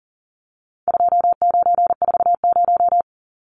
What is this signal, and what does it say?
Morse code, heard as a keyed tone, about 40 wpm, 700 Hz. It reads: 2940
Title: Numbers - 4-Digit 40wpm